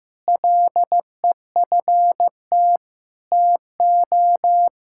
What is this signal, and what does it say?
Morse code, heard as a keyed tone, about 15 wpm, 700 Hz. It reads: LEFT TO
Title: Sentences from Top 1000 Words - Review Part 5 - 15wpm